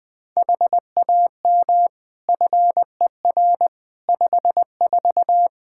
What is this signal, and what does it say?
Morse code, heard as a keyed tone, about 21 wpm, 700 Hz. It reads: HAM FER 54
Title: K1USN SST QSO Practice 20wpm